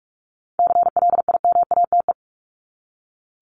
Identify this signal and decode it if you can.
Morse code, heard as a keyed tone, about 44 wpm, 700 Hz. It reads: CLIMATE
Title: Top 1501-1750 Words 45wpm